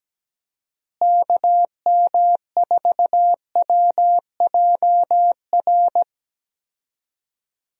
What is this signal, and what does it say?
Morse code, heard as a keyed tone, about 17 wpm, 700 Hz. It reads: KM4WJR
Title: USA Call Signs - 6 Characters - Repeated 2x - 17wpm